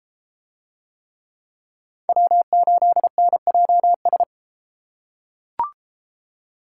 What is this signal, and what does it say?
Morse code, heard as a keyed tone, about 33 wpm, 700 Hz. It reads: W8NJS